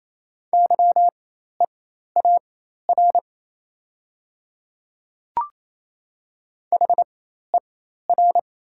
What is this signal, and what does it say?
Morse code, heard as a keyed tone, about 27 wpm, spaced as for 13 wpm, 700 Hz. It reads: YEAR HER
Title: 4x Character Spacing - Top 500 Words - 28wpm